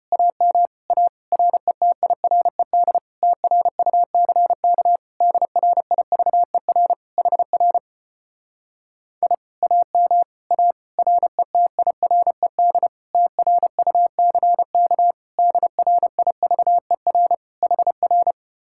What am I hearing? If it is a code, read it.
Morse code, heard as a keyed tone, about 31 wpm, 700 Hz. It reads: AM A RETIRED TRUCK DRIVER HR I AM A RETIRED TRUCK DRIVER HR
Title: Speed-Racing Common QSO Phrases - 25wpm